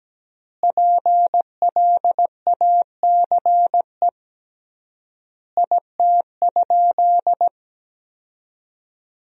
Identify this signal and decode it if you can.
Morse code, heard as a keyed tone, about 17 wpm, 700 Hz. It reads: PLACE IT?
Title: 3x Word-Spacing - Sentences from Top 200 Words - 17wpm